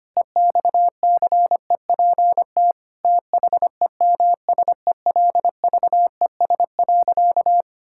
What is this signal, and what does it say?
Morse code, heard as a keyed tone, about 25 wpm, 700 Hz. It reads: EXCEPT THEMSELVES.
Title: Sentences from Top 1600 Words 25wpm